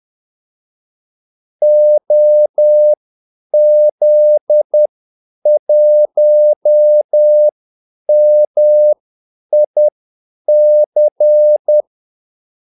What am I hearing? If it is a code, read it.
Morse code, heard as a keyed tone, about 10 wpm, spaced as for 8 wpm, 600 Hz. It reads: OZ1MIC
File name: Callsigns-10-8wpm